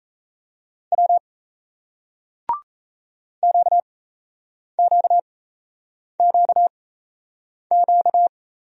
Morse code, heard as a keyed tone, about 38 wpm, spaced as for 8 wpm, 700 Hz. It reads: W QQQQ